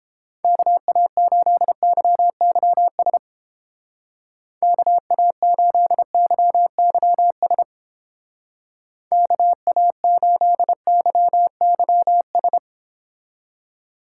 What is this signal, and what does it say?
Morse code, heard as a keyed tone, about 30 wpm, 700 Hz. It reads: KA8YYS KA8YYS KA8YYS